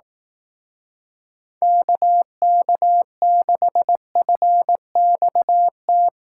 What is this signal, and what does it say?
Morse code, heard as a keyed tone, about 18 wpm, 700 Hz. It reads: KK6FXT
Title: Speed Racing - USA Call Signs 15wpm